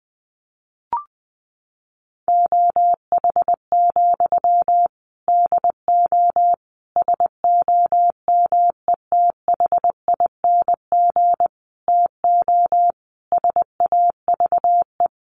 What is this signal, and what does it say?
Morse code, heard as a keyed tone, about 20 wpm, 700 Hz. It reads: OH, DO SOMETHING TO SAVE
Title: Sentences from Top 1100 Words 20wpm